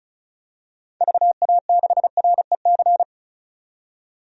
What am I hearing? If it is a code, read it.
Morse code, heard as a keyed tone, about 34 wpm, 700 Hz. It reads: VA6REC
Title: DXCC-100 Call Signs - 6 Characters - 35wpm